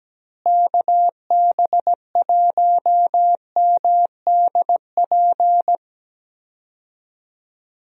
Morse code, heard as a keyed tone, about 17 wpm, 700 Hz. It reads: KB1MDP
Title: USA Call Signs - 6 Characters - Repeated 2x - 17wpm